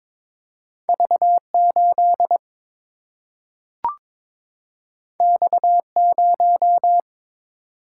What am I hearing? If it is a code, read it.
Morse code, heard as a keyed tone, about 22 wpm, 700 Hz. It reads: V 8 X 0
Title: Single-Letter Single-Number 22wpm